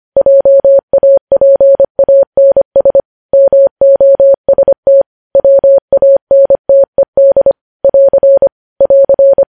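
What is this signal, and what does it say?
Morse code, heard as a keyed tone, about 25 wpm, 550 Hz. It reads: JAPANS MOST WANTED <AR> <AR>
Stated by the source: News headlines sent in Morse code at 25 wpm.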